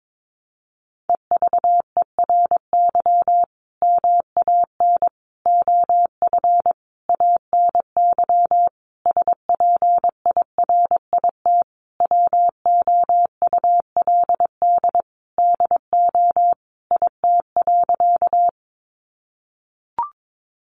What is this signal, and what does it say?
Morse code, heard as a keyed tone, about 22 wpm, 700 Hz. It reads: EVERY MAN OF ANY SPIRIT WOULD DO IT.
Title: Sentences from Top 1400 Words 22wpm